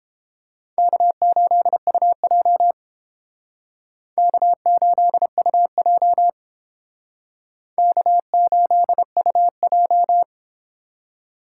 Morse code, heard as a keyed tone, about 30 wpm, 700 Hz. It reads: K8UJ K8UJ K8UJ